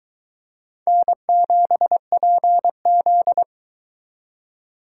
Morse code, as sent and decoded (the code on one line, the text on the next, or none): -. --... .--. --..
N7PZ